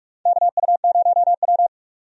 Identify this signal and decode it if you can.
Morse code, heard as a keyed tone, about 42 wpm, 700 Hz. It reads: KU0W
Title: USA Call Signs - 4 Characters - 45wpm